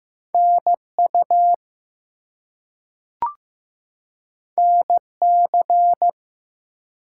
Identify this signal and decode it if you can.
Morse code, heard as a keyed tone, about 15 wpm, 700 Hz. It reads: NU NC
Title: US States and Canadian Provinces 15wpm